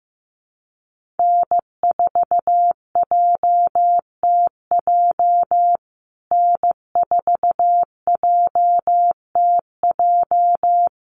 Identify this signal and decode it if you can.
Morse code, heard as a keyed tone, about 15 wpm, 700 Hz. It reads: N4JTJ N4JTJ
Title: USA Call Signs Repeated 3x 15wpm